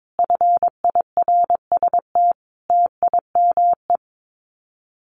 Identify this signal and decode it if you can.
Morse code, heard as a keyed tone, about 22 wpm, 700 Hz. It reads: FIRST TIME
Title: Common QSO Phrases 22wpm